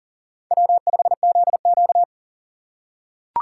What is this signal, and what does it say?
Morse code, heard as a keyed tone, about 40 wpm, 700 Hz. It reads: W5ZQ
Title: USA Call Signs - 4 Characters - 3x Character Spacing - 40wpm